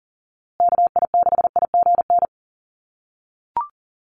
Morse code, heard as a keyed tone, about 40 wpm, 700 Hz. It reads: KI6IGN
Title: USA Call Signs Repeated 2x 40wpm